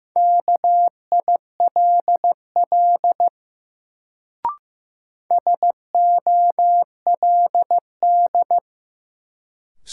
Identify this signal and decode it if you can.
Morse code, heard as a keyed tone, about 15 wpm, 700 Hz. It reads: KILL SOLD
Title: Sentences from Top 1400 Words - New Word Drill - 15wpm